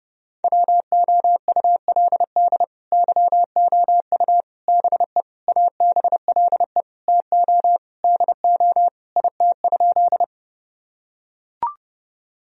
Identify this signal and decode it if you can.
Morse code, heard as a keyed tone, about 32 wpm, 700 Hz. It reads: WOULD YOU BE ABLE TO DO IT?
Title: Sentences from Top 1000 Words - Review Part 2 - 30wpm